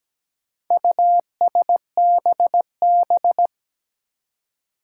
Morse code, heard as a keyed tone, about 17 wpm, 700 Hz. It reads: USBB
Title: Four Challenging Letter-Number Combinations 17wpm